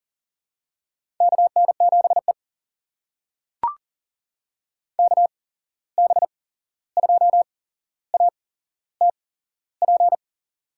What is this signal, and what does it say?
Morse code, heard as a keyed tone, about 38 wpm, spaced as for 15 wpm, 700 Hz. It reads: KN7E KB2ATP
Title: USA Call Signs - 8x Character Spacing - 40wpm